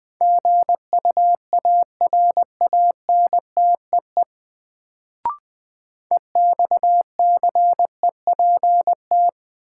Morse code, heard as a keyed tone, about 20 wpm, 700 Hz. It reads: GUARANTEE EXCEPT
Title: Sentences from Top 1600 Words - New Word Drill Alt - 20wpm